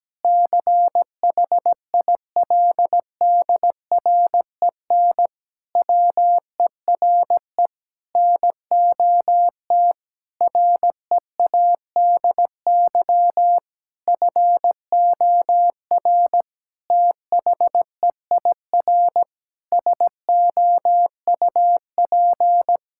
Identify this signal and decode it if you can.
Morse code, heard as a keyed tone, about 17 wpm, 700 Hz. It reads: CHILDREN WERE NOT READY FOR THEIR SOUP
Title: Sentences from Top 1000 Words - Review Part 6 - 17wpm